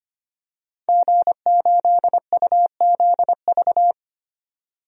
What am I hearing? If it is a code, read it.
Morse code, heard as a keyed tone, about 25 wpm, 700 Hz. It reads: G8UZV
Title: DXCC-100 Call Signs - 5 Characters - 25wpm